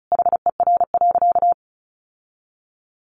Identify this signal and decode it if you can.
Morse code, heard as a keyed tone, about 34 wpm, 700 Hz. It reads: HER.
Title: Sentences from Top 1500 Words 35wpm